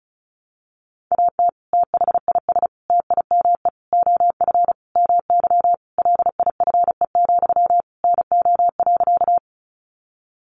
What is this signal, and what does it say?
Morse code, heard as a keyed tone, about 34 wpm, 700 Hz. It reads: AT THIS TIME OF MY LIFE, NO.
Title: Sentences from Top 300 Words 35wpm